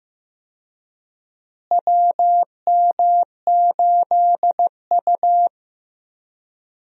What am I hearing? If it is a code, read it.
Morse code, heard as a keyed tone, about 15 wpm, 700 Hz. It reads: WM8U
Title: USA Call Signs - 4 Characters - Repeated 3x - 15wpm